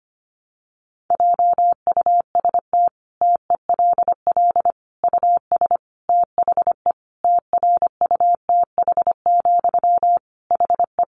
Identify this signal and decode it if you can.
Morse code, heard as a keyed tone, about 25 wpm, 700 Hz. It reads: JUST TELL US THE TRUTH, HE
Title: Sentences from Top 800 Words 25wpm